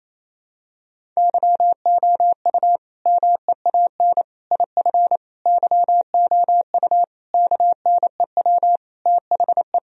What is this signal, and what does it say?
Morse code, heard as a keyed tone, about 28 wpm, 700 Hz. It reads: YOU MEAN IF YOU KNEW THE
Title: Sentences from Top 1000 Words - Review Part 4 - 28wpm